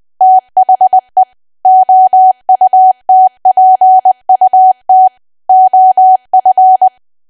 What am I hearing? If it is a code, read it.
Morse code, heard as a keyed tone, about 20 wpm, 750 Hz. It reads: THE OUTPUT OF
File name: cwa / W1AW-Level3 / 20WPM